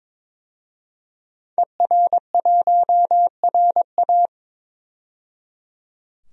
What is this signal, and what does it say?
Morse code, heard as a keyed tone, about 22 wpm, 700 Hz. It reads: ER1RA
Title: DXCC-100 Call Signs - 5 Characters - 22wpm